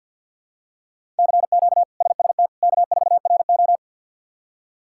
Morse code, heard as a keyed tone, about 48 wpm, 700 Hz. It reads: CQ SST K4RO
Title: K1USN SST QSO Practice - 50wpm